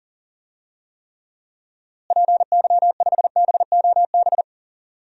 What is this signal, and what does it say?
Morse code, heard as a keyed tone, about 37 wpm, 700 Hz. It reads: PY5BOB
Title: DXCC-100 Call Signs - 6 Characters - Repeated 2x - 40wpm